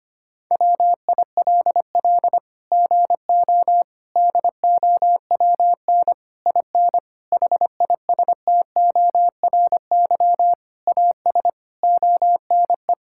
Morse code, heard as a keyed tone, about 25 wpm, 700 Hz. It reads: WILL GO DOWN IN HISTORY AS ONE
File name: Sets of 9 Words - 25wpm